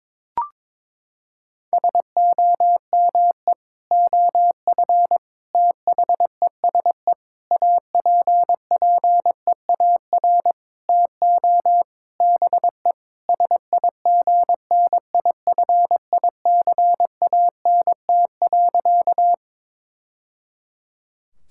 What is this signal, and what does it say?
Morse code, heard as a keyed tone, about 22 wpm, 700 Hz. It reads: SOME OF THESE APPEAR TO BE SIGNIFICANT.
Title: Sentences from Top 1000 Words - Review Part 6 - 22wpm